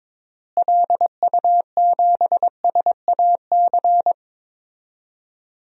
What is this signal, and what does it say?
Morse code, heard as a keyed tone, about 22 wpm, 700 Hz. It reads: LU7SAC
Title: DXCC-100 Call Signs - 6 Characters - Repeated 3x - 22wpm